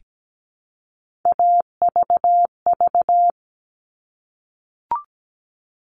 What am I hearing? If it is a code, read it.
Morse code, heard as a keyed tone, about 17 wpm, 700 Hz. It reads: AVV
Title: All Cut Numbers - 3-Digit 17wpm